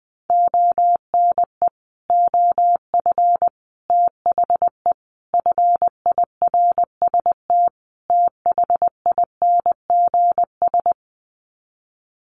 Morse code, heard as a keyed tone, about 20 wpm, 700 Hz. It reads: ONE OF THE FIRST THINGS
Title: Sets of 5 Words Encore 20wpm